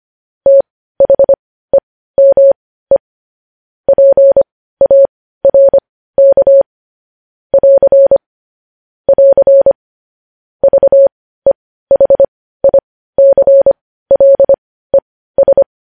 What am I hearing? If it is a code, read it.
Morse code, heard as a keyed tone, about 25 wpm, spaced as for 15 wpm, 550 Hz. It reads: THEME PARK <AR> <AR> VEHICLES
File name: podcast / News headlines-1x-25x15